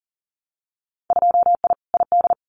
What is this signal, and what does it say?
Morse code, heard as a keyed tone, about 38 wpm, 700 Hz. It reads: 2I ID
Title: Winter Field Day Exchanges Encore 40wpm